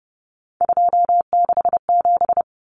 Morse code, heard as a keyed tone, about 30 wpm, 700 Hz. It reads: 267
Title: Common Cut Numbers - 3-Digit 30wpm